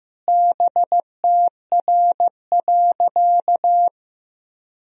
Morse code, heard as a keyed tone, about 15 wpm, 700 Hz. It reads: BTR.
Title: Sentences from Top 400 Words and Abbreviations 15wpm